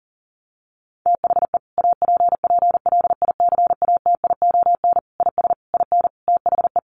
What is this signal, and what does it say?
Morse code, heard as a keyed tone, about 40 wpm, 700 Hz. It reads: THE APPLICATION IS IN THE
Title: Sentences from Top 1200 Words 40wpm